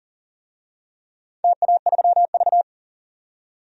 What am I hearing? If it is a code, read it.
Morse code, heard as a keyed tone, about 37 wpm, 700 Hz. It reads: TA3V
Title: DXCC-100 Call Signs - 4 Characters - 40wpm